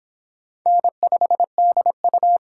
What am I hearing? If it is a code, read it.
Morse code, heard as a keyed tone, about 25 wpm, 700 Hz. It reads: N5DU